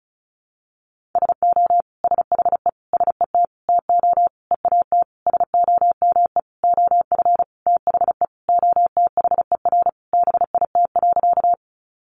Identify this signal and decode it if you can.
Morse code, heard as a keyed tone, about 34 wpm, 700 Hz. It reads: SO SHE SET TO EAT SOME OF THE OTHER BIT.
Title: Sentences from Top 900 Words 35wpm